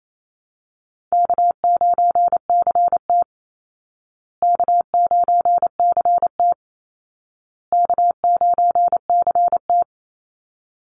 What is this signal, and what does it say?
Morse code, heard as a keyed tone, about 28 wpm, 700 Hz. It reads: K9CT K9CT K9CT
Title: Easy Call Signs Repeated 3x - 28wpm